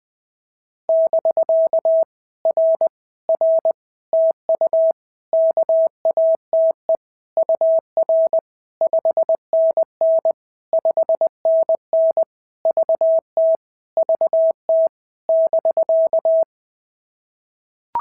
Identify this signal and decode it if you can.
Morse code, heard as a keyed tone, about 20 wpm, 690 Hz. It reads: <BK> R R TU KATE UR 5NN 5NN VT VT <BK>
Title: POTA QSO Practice - 20wpm